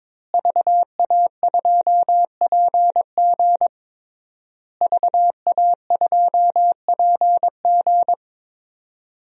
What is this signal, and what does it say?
Morse code, heard as a keyed tone, about 22 wpm, 700 Hz. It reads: VA2PG VA2PG
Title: DXCC-100 Call Signs - 5 Characters - Repeated 3x - 22wpm